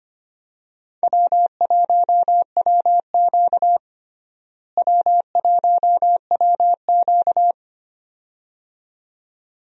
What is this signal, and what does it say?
Morse code, heard as a keyed tone, about 25 wpm, 700 Hz. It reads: W1WQ W1WQ
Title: USA Call Signs - 4 Characters - Repeated 2x - 25wpm